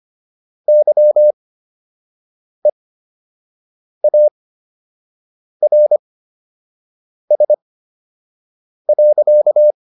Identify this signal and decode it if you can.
Morse code, heard as a keyed tone, about 25 wpm, spaced as for 6 wpm, 600 Hz. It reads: YEARS.